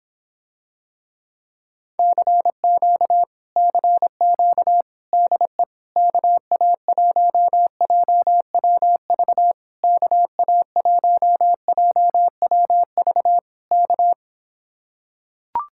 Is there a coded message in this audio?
Yes: Morse code from 2.0 to 14.1 s, heard as a keyed tone, about 26 wpm, 700 Hz. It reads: CQ CQ DE KA1JWV KA1JWV K